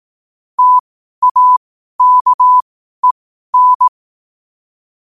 Morse code, heard as a keyed tone, about 18 wpm, spaced as for 12 wpm, 1000 Hz. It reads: TAKEN
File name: qotdfiles / 12wpm